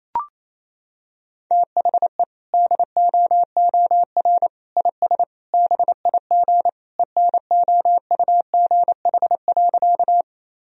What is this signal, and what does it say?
Morse code, heard as a keyed tone, about 28 wpm, 700 Hz. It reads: THE DOOR IS BIG ENOUGH.
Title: Sentences from Top 1000 Words - Review Part 1 - 28wpm